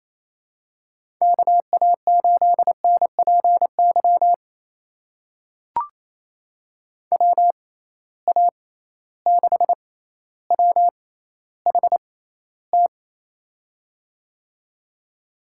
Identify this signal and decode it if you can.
Morse code, heard as a keyed tone, about 28 wpm, spaced as for 14 wpm, 700 Hz. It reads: KA8NPY WA6WHT
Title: USA Call Signs - 6 Characters - 6x Character Spacing - 28wpm